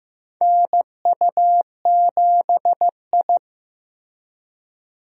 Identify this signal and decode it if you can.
Morse code, heard as a keyed tone, about 16 wpm, 700 Hz. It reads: NU7I
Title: USA Call Signs - 4 Characters - Repeated 2x - 15wpm